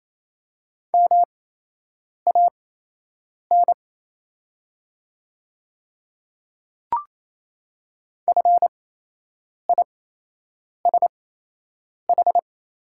Morse code, heard as a keyed tone, about 28 wpm, spaced as for 8 wpm, 700 Hz. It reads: MAN FISH
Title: 8x Character Spacing - Top 300 Words - 28wpm